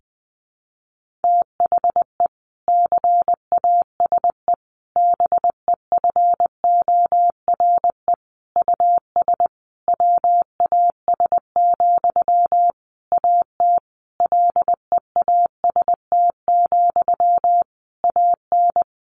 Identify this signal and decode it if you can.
Morse code, heard as a keyed tone, about 20 wpm, 700 Hz. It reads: THE CASE BEFORE US WAS, AT LEAST, AN
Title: Sentences from Top 1400 Words 20wpm